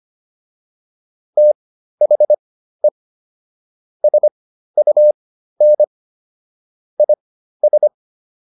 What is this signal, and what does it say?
Morse code, heard as a keyed tone, about 25 wpm, spaced as for 13 wpm, 600 Hz. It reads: THE SUN IS